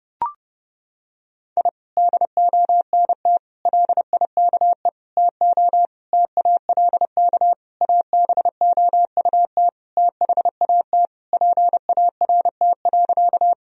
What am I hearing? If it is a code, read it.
Morse code, heard as a keyed tone, about 30 wpm, 700 Hz. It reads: I DONT LIKE TO TALK ABOUT THAT PART.
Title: Sentences from Top 1000 Words - Review Part 1 - 30wpm